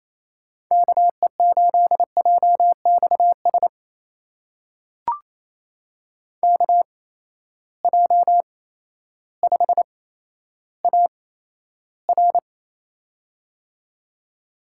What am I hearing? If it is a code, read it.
Morse code, heard as a keyed tone, about 27 wpm, spaced as for 13 wpm, 700 Hz. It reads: KE8JXS KJ5AR
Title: USA Call Signs - 8x Character Spacing - 28wpm